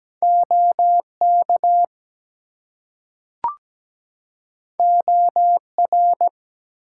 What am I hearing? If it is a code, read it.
Morse code, heard as a keyed tone, about 17 wpm, 700 Hz. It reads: OK OR
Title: Two Letter Words - 17wpm